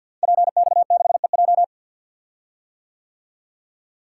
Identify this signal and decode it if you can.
Morse code, heard as a keyed tone, about 46 wpm, 700 Hz. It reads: PX6EJ
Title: Code Groups - 5 Letter:Numbers - 50wpm